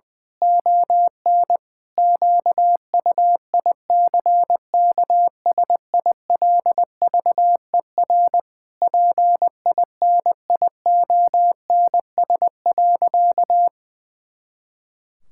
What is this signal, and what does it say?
Morse code, heard as a keyed tone, about 20 wpm, 700 Hz. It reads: ON QUICKSILVER PINIONS.
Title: Pangram Challenge v2 - 20wpm